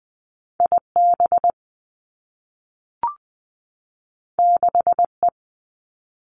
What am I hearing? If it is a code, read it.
Morse code, heard as a keyed tone, about 20 wpm, 700 Hz. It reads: I B 6 E
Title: Two Challenging Letter-Number Combinations 20wpm